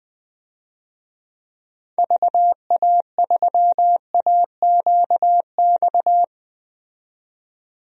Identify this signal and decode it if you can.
Morse code, heard as a keyed tone, about 20 wpm, 700 Hz. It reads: VA3AQX
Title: DXCC-100 Call Signs - 6 Characters - 20wpm